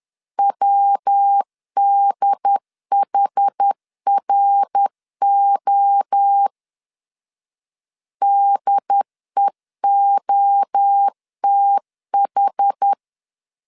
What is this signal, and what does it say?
Morse code, heard as a keyed tone, about 10 wpm, 800 Hz. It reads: WDHRO DEOTH
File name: CW / DET / 1 3 1500C S C 10wpm 3335